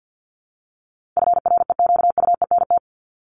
Morse code, heard as a keyed tone, about 49 wpm, 700 Hz. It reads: FREQUENT
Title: LY Suffix - Top 100 Words with Base Words 50wpm